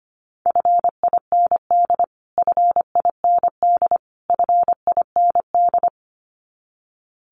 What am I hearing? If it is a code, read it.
Morse code, heard as a keyed tone, about 25 wpm, 700 Hz. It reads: FIND FIND FIND
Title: Top 500 Words Repeated 25wpm